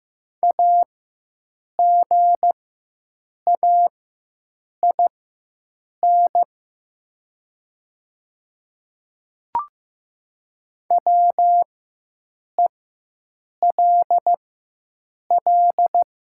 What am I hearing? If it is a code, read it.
Morse code, heard as a keyed tone, about 15 wpm, spaced as for 7 wpm, 700 Hz. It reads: AGAIN WELL
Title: 4x Character Spacing - Top 200 Words - 15wpm